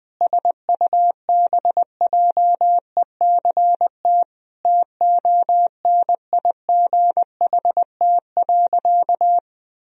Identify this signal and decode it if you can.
Morse code, heard as a keyed tone, about 20 wpm, 700 Hz. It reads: SUBJECT TONIGHT.
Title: Sentences from Top 1000 Words - Review Part 4 - 20wpm